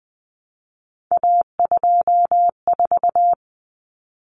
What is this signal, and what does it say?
Morse code, heard as a keyed tone, about 20 wpm, 700 Hz. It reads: A24
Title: Common Cut Numbers - 3-Digit 20wpm